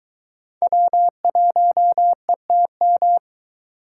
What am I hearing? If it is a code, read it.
Morse code, heard as a keyed tone, about 22 wpm, 700 Hz. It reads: W1ETM